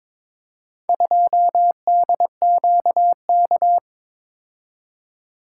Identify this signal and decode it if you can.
Morse code, heard as a keyed tone, about 22 wpm, 700 Hz. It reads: 2DQK
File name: practice / 22wpm / Four Letter-Number Combinations - 22wpm